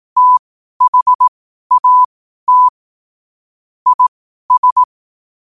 Morse code, heard as a keyed tone, about 18 wpm, spaced as for 12 wpm, 1000 Hz. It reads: THAT IS
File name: qotdfiles / 12wpm